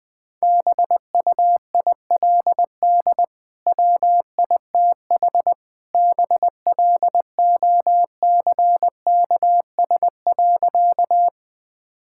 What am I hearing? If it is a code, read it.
Morse code, heard as a keyed tone, about 20 wpm, 700 Hz. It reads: BUILD WITH BLOCKS.
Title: Easy Sentences with Hints - 20wpm